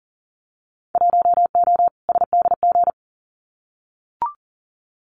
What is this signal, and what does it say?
Morse code, heard as a keyed tone, about 40 wpm, 700 Hz. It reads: 1O SDG
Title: Winter Field Day Exchanges Encore 40wpm